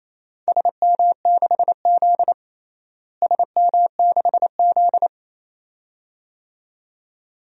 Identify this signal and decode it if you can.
Morse code, heard as a keyed tone, about 28 wpm, 700 Hz. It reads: SM6Z SM6Z
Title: DXCC-100 Call Signs - 4 Characters - Repeated 2x - 28wpm